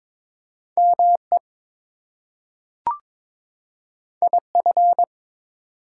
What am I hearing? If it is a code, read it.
Morse code, heard as a keyed tone, about 22 wpm, 700 Hz. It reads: ME IF
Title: Two Letter Words - 22wpm